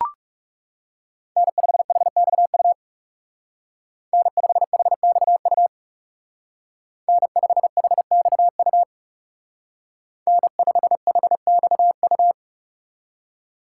Morse code, heard as a keyed tone, about 39 wpm, 700 Hz. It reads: N5HXU N5HXU N5HXU N5HXU